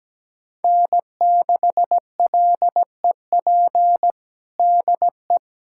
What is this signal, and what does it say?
Morse code, heard as a keyed tone, about 17 wpm, 700 Hz. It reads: N6LEP DE
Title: USA Callsign DE Callsign - 5 Characters - 17wpm